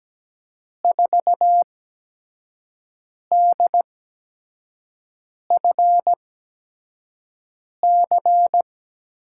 Morse code, heard as a keyed tone, about 17 wpm, spaced as for 5 wpm, 700 Hz. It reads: 4DFC
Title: USA Call Signs - 6 Characters - 8x Character Spacing - 17wpm